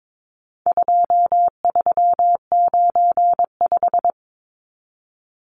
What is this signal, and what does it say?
Morse code, heard as a keyed tone, about 22 wpm, 700 Hz. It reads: 2395
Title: Numbers - 4-Digit 22wpm